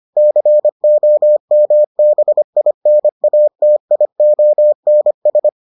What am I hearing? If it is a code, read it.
Morse code, heard as a keyed tone, about 25 wpm, 600 Hz. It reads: COMBINATIONS